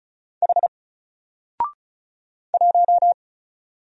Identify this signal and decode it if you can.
Morse code, heard as a keyed tone, about 35 wpm, 700 Hz. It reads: H 1
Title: Single Character (Letter:Number:Punctuation:Prosign) 35wpm